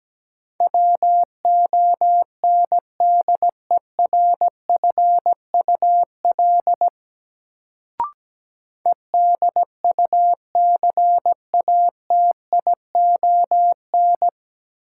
Morse code, heard as a keyed tone, about 17 wpm, 700 Hz. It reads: WONDERFUL EDUCATION
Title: Sentences from Top 1100 Words - New Word Drill Alt - 17wpm